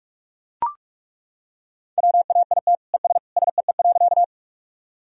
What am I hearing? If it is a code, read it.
Morse code, heard as a keyed tone, about 41 wpm, 700 Hz. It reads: WAIT ES SEE.
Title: Sentences from Top 500 Words and Abbreviations 45wpm